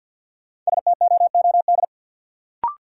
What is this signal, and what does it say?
Morse code, heard as a keyed tone, about 53 wpm, 700 Hz. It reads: STOOD
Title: Sentences from Top 500 Words - New Word Drill - 50wpm